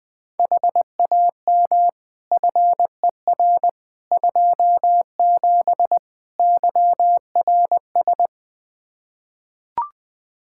Morse code, heard as a keyed tone, about 20 wpm, 700 Hz. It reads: HAM FER 27 YRS
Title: Practice QSO - Exchange 3 - More Info - 20wpm